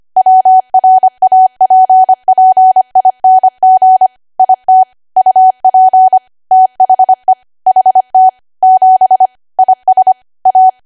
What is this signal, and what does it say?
Morse code, heard as a keyed tone, about 25 wpm, 750 Hz. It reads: WRAPPING IT UP THE HT 7 IS A
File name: cwa / W1AW-Level3 / 25WPM